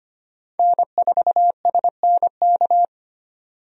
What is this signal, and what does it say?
Morse code, heard as a keyed tone, about 25 wpm, 700 Hz. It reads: N4SNK
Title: USA Call Signs - 5 Characters - 25wpm